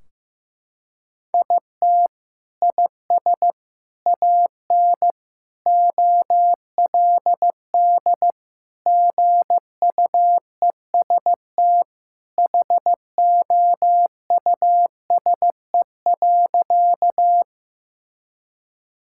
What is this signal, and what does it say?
Morse code, heard as a keyed tone, about 15 wpm, 700 Hz. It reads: IT IS AN OLD GUEST HOUSE.
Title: Sentences from Top 1600 Words 15wpm